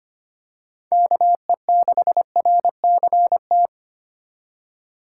Morse code, heard as a keyed tone, about 25 wpm, 700 Hz. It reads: KE6RCT
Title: USA Call Signs - 6 Characters - 3x Character Spacing - 25wpm